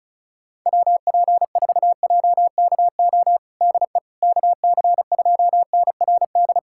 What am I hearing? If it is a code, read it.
Morse code, heard as a keyed tone, about 34 wpm, 700 Hz. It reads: WP4JKO DE KC2NRD
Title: USA Callsign DE Callsign 35wpm